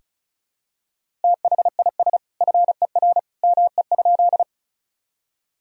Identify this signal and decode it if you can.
Morse code, heard as a keyed tone, about 36 wpm, 700 Hz. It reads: THIS FER ME?
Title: Sentences from Top 200 Words and Abbreviations 35wpm